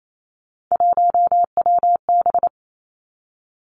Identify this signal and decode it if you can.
Morse code, heard as a keyed tone, about 28 wpm, 700 Hz. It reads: 1WB
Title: Three Challenging Letter-Number Combinations 28wpm